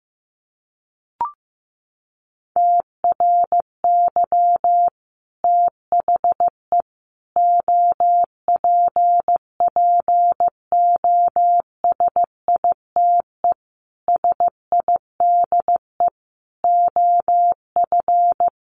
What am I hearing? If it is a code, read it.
Morse code, heard as a keyed tone, about 15 wpm, 700 Hz. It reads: TRY THE OPPOSITE SIDE OF
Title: Sentences from Top 1300 Words 15wpm